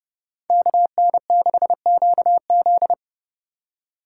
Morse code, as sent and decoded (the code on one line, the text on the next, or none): -.- -. -.... --.- --..
KN6QZ